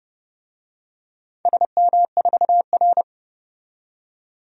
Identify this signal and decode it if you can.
Morse code, heard as a keyed tone, about 31 wpm, 700 Hz. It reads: SM4R
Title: DXCC-100 Call Signs - 4 Characters - 30wpm